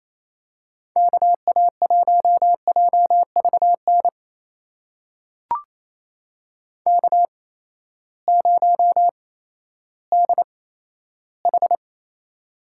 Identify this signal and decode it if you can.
Morse code, heard as a keyed tone, about 28 wpm, spaced as for 14 wpm, 700 Hz. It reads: KA1JVN K0DH
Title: USA Call Signs - 8x Character Spacing - 28wpm